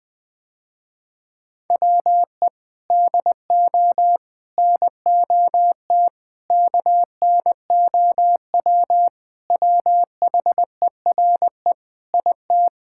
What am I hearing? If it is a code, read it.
Morse code, heard as a keyed tone, about 21 wpm, 700 Hz. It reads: WE DO NOT KNOW WHERE IT
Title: Sentences with 4-Word Phrases - Spoken Beginning - 20wpm